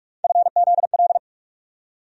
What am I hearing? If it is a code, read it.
Morse code, heard as a keyed tone, about 41 wpm, 700 Hz. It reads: FZL
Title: USA Call Sign Postfixes - 3 Letters 45wpm